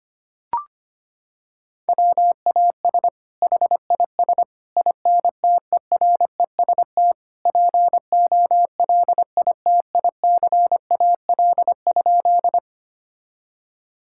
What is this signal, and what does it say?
Morse code, heard as a keyed tone, about 25 wpm, 700 Hz. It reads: WAS HIS INTEREST POLITICAL?
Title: Sentences from Top 1000 Words - Review Part 5 - 25wpm